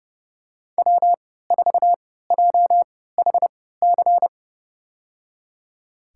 Morse code, heard as a keyed tone, about 30 wpm, spaced as for 17 wpm, 700 Hz. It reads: W4JHC
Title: USA Call Signs - 5 Characters - 3x Character Spacing - 30wpm